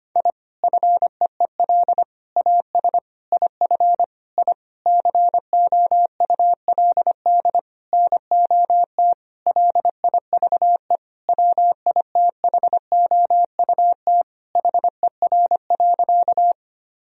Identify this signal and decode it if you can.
Morse code, heard as a keyed tone, about 25 wpm, 700 Hz. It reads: I FEEL AS IF I COULD NOT LIVE WITHOUT HER.
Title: Sentences from Top 1000 Words - Review Part 3 - 25wpm